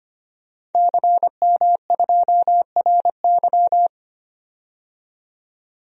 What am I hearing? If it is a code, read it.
Morse code, heard as a keyed tone, about 25 wpm, 700 Hz. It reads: CM2RY
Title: DXCC-100 Call Signs - 5 Characters - 25wpm